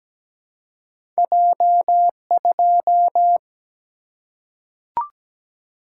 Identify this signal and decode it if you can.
Morse code, heard as a keyed tone, about 17 wpm, 700 Hz. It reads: J2
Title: Single-Letter Single-Number 17wpm